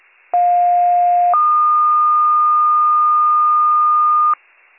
Two-tone page, heard as a one-second tone, then a three-second tone, about 710 then 1220 Hz. It is heard from 0.3 to 4.3 s.